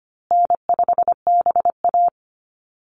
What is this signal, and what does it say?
Morse code, heard as a keyed tone, about 25 wpm, 700 Hz. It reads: N5BA
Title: Easy Call Signs Repeated 3x - 25wpm